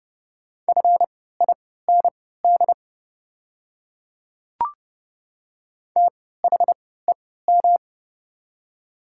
Morse code, heard as a keyed tone, about 30 wpm, spaced as for 17 wpm, 700 Hz. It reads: FIND THEM
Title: 3x Character Spacing - Top 100 Words - 30wpm